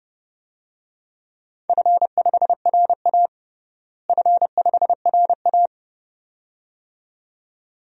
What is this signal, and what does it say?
Morse code, heard as a keyed tone, about 32 wpm, 700 Hz. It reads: F5RA F5RA
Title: DXCC-100 Call Signs - 4 Characters - Repeated 2x - 30wpm